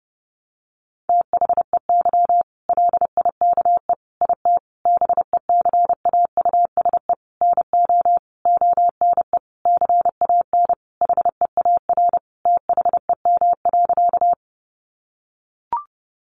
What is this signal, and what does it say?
Morse code, heard as a keyed tone, about 30 wpm, 700 Hz. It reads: THEY LIKE IT BECAUSE NO ONE CAN HEAR THEM.
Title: Sentences from Top 600 Words 30wpm